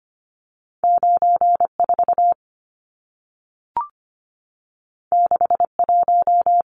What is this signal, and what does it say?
Morse code, heard as a keyed tone, about 25 wpm, 700 Hz. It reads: 94 61
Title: Numbers - 2-Digit 25wpm